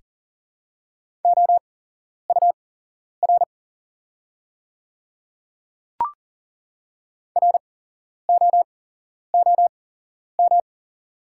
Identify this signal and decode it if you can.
Morse code, heard as a keyed tone, about 40 wpm, spaced as for 11 wpm, 690 Hz. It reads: OUR ROOM
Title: 8x Character Spacing - Top 300 Words - 40wpm